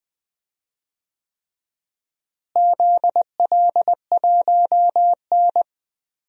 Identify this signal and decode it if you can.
Morse code, heard as a keyed tone, about 20 wpm, 700 Hz. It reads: ZL1N
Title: DXCC-100 Call Signs - 4 Characters - Repeated 3x - 20wpm